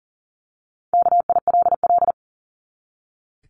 Morse code, heard as a keyed tone, about 40 wpm, 700 Hz. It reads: KILL
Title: Top 1251-1500 Words 40wpm